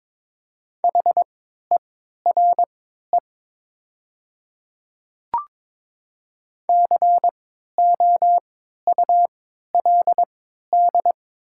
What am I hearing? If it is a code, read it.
Morse code, heard as a keyed tone, about 22 wpm, spaced as for 12 wpm, 700 Hz. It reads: HERE COULD
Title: 3x Character Spacing - Top 200 Words - 22wpm